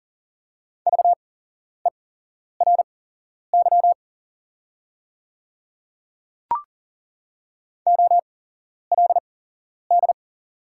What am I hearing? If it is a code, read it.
Morse code, heard as a keyed tone, about 40 wpm, spaced as for 11 wpm, 700 Hz. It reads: VERY OLD
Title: 8x Character Spacing - Top 200 Words - 40wpm